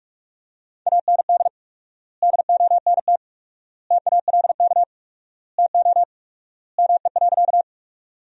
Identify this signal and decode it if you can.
Morse code, heard as a keyed tone, about 47 wpm, 700 Hz. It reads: AND DONT TALK TO ME.
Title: 4x Word-Spacing - Sentences from Top 400 Words - 45wpm